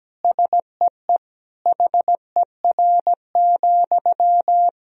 Morse code, heard as a keyed tone, about 17 wpm, 700 Hz. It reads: SEE HER,
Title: Sentences from Top 1000 Words - Review Part 3 - 17wpm